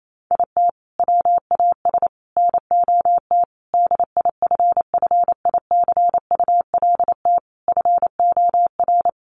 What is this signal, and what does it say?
Morse code, heard as a keyed tone, about 28 wpm, 700 Hz. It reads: IT WAS NOT DIFFICULT FOR
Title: Sentences from Top 1200 Words 28wpm